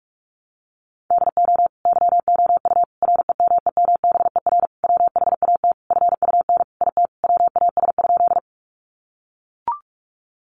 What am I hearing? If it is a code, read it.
Morse code, heard as a keyed tone, about 44 wpm, 700 Hz. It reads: DO YOU REMEMBER WHAT FUN IT WAS?
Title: Sentences from Top 700 Words 45wpm